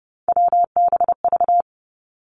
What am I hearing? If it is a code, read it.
Morse code, heard as a keyed tone, about 30 wpm, 700 Hz. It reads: WBV
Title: Three Challenging Letter-Number Combinations 30wpm